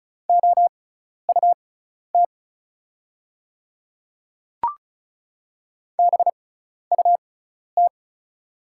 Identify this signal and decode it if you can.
Morse code, heard as a keyed tone, about 36 wpm, spaced as for 12 wpm, 700 Hz. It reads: OUT BUT
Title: 6x Character Spacing - Top 100 Words - 35wpm